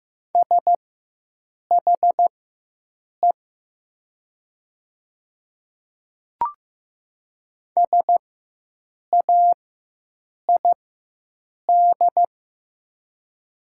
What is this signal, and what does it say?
Morse code, heard as a keyed tone, about 15 wpm, spaced as for 7 wpm, 700 Hz. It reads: SHE SAID
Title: 4x Character Spacing - Top 200 Words - 15wpm